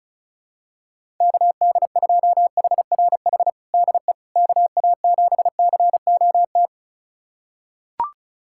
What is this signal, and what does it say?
Morse code, heard as a keyed tone, about 36 wpm, 700 Hz. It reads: KD2HRH DE KA7COT
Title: USA Callsign DE Callsign 35wpm